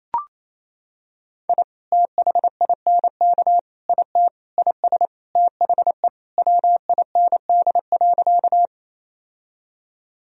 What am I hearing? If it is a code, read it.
Morse code, heard as a keyed tone, about 28 wpm, 700 Hz. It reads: I THINK IT IS THE WIND.
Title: Sentences from Top 1000 Words - Review Part 2 - 28wpm